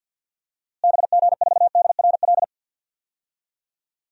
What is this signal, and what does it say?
Morse code, heard as a keyed tone, about 47 wpm, 700 Hz. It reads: BG4DRL
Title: DXCC-100 Call Signs - 6 Characters - Repeated 3x - 50wpm